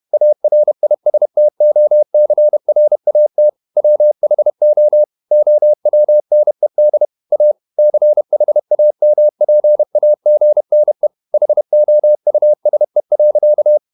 Morse code, heard as a keyed tone, about 31 wpm, 600 Hz. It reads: ARISTOCRAT WHO OWNED A CHAMPAGNE HOUSE.
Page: Never 31wpm | CW med Gnister